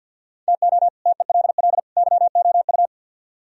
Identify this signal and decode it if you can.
Morse code, heard as a keyed tone, about 66 wpm, spaced as for 57 wpm, 700 Hz. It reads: TO TELL YOU
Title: Speed-Racing - Sets of 3 Words 50wpm